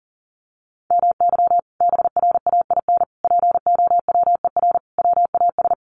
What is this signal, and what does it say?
Morse code, heard as a keyed tone, about 40 wpm, 700 Hz. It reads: MY BRAIN POWER WAS
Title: Sentences from Top 1400 Words 40wpm